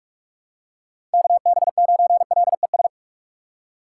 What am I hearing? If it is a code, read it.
Morse code, heard as a keyed tone, about 46 wpm, 700 Hz. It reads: KB9LES
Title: USA Call Signs - 6 Characters - 2x Character Spacing - 45wpm